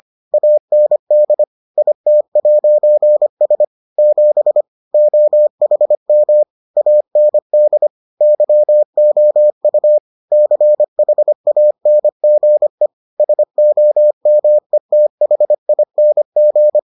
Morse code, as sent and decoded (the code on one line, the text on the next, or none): .- -. -.. / .. - .----. ... / --... / --- .... -- / .- -. -.. / -.-- --- ..- / -.-. .... .- -. --. / ... --- -- . - .... .. -. --.
AND IT'S 7 OHM AND YOU CHANGE SOMETHING